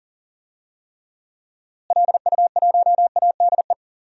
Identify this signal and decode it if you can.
Morse code, heard as a keyed tone, about 40 wpm, 700 Hz. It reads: LU1ADE
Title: DXCC-100 Call Signs - 6 Characters - Repeated 2x - 40wpm